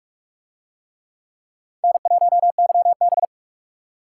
Morse code, heard as a keyed tone, about 44 wpm, 700 Hz. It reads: N1YB